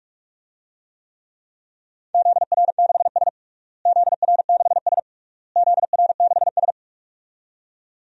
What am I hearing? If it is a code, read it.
Morse code, heard as a keyed tone, about 41 wpm, 700 Hz. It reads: ZR6S ZR6S ZR6S
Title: DXCC-100 Call Signs - 4 Characters - Repeated 3x - 45wpm